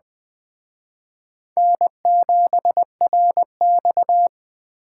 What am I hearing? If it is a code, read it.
Morse code, heard as a keyed tone, about 21 wpm, 700 Hz. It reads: N7RX